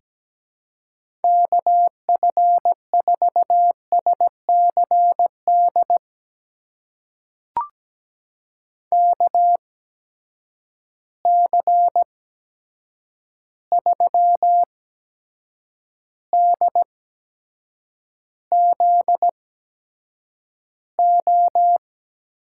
Morse code, heard as a keyed tone, about 17 wpm, spaced as for 7 wpm, 700 Hz. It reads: KF4SCD KC3DZO
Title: USA Call Signs - 6 Characters - 8x Character Spacing - 17wpm